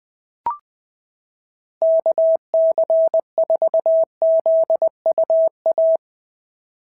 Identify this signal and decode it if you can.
Morse code, heard as a keyed tone, about 18 wpm, 650 Hz. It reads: KC4ZUA
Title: POTA QSO Practice - 20wpm